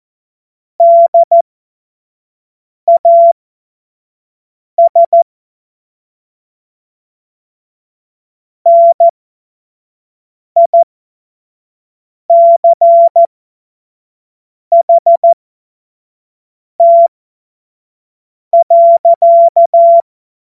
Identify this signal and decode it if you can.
Morse code, heard as a keyed tone, about 14 wpm, spaced as for 5 wpm, 680 Hz. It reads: DAS NICHT.
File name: Morsekurs / Witze / Witz 14. März 2023 25 F70